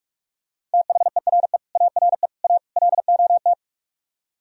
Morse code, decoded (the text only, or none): THERE ARE A LOT